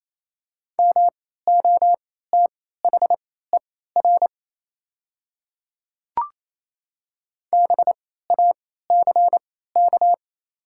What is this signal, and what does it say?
Morse code, heard as a keyed tone, about 28 wpm, spaced as for 16 wpm, 690 Hz. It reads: MOTHER BACK
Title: 3x Character Spacing - Top 500 Words - 28wpm